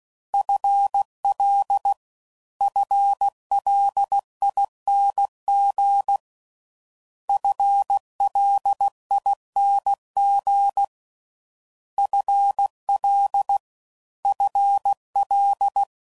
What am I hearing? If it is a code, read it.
Morse code, heard as a keyed tone, about 16 wpm, 800 Hz. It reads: FL FLING FLING FL FL